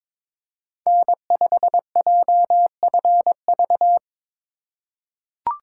Morse code, heard as a keyed tone, about 22 wpm, 700 Hz. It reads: N5JFV
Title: USA Call Signs - 5 Characters - 3x Character Spacing - 22wpm